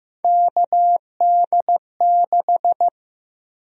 Morse code, heard as a keyed tone, about 15 wpm, 700 Hz. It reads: KD6
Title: USA Call Sign Prefixes - 15wpm